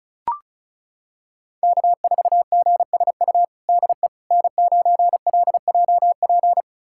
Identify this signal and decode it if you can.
Morse code, heard as a keyed tone, about 34 wpm, 700 Hz. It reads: K4GSU DE N9LJP
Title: USA Callsign DE Callsign 35wpm